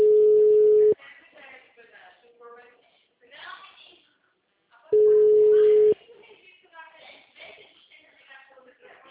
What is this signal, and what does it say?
Ringback tone (Europe, 425 Hz), with speech.